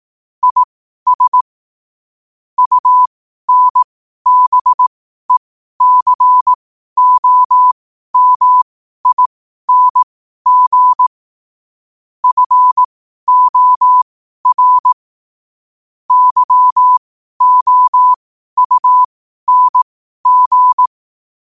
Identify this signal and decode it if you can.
Morse code, heard as a keyed tone, about 18 wpm, spaced as for 12 wpm, 1000 Hz. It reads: IS UNBECOMING FOR YOUNG
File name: qotdfiles / 12wpm